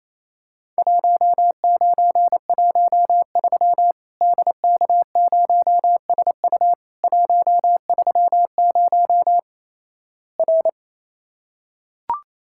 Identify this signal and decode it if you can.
Morse code, heard as a keyed tone, about 28 wpm, 700 Hz. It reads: 1913 DK0SU 130 R
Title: Worked All Europe DX Contest - QTC Exchange - 28wpm